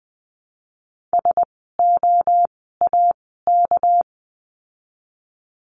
Morse code, heard as a keyed tone, about 20 wpm, spaced as for 14 wpm, 700 Hz. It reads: SOAK
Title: 199 Lesson 32f - Letter k - Words 20wpm